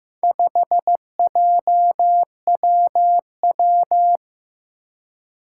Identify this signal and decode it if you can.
Morse code, heard as a keyed tone, about 15 wpm, 700 Hz. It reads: 5JWW
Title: Four Challenging Letter-Number Combinations 15wpm